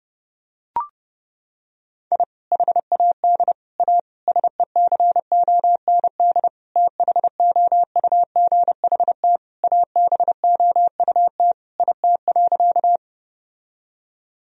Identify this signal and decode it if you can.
Morse code, heard as a keyed tone, about 29 wpm, 700 Hz. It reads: I HAD A SECOND THOUGHT ABOUT IT.
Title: Sentences from Top 1000 Words - Review Part 1 - 30wpm